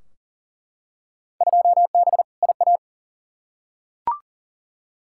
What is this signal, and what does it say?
Morse code, heard as a keyed tone, about 40 wpm, 700 Hz. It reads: 2B IA
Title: Field Day Exchanges Encore - 40wpm